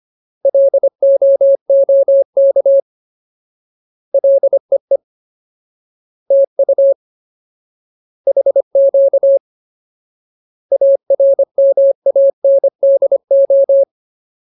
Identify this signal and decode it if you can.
Morse code, heard as a keyed tone, about 25 wpm, 550 Hz. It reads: LOOK LEE TU HQ ARMANDO
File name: Ultimate Word List / 25WPM Ultimate